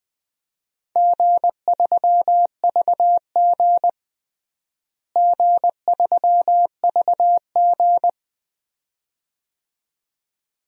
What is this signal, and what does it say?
Morse code, heard as a keyed tone, about 20 wpm, 700 Hz. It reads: G3VG G3VG
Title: DXCC-100 Call Signs - 4 Characters - Repeated 2x - 20wpm